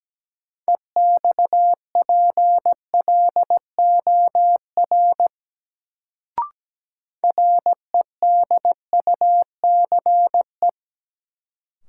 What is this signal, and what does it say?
Morse code, heard as a keyed tone, about 17 wpm, 700 Hz. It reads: EXPLOR REDUCE
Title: Sentences from Top 1100 Words - New Word Drill - 17wpm